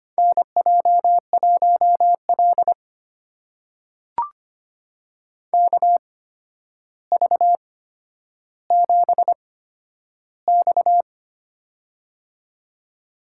Morse code, heard as a keyed tone, about 25 wpm, spaced as for 11 wpm, 700 Hz. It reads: NJ1L KV7X
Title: USA Call Signs - 4 Characters - 8x Character Spacing - 25wpm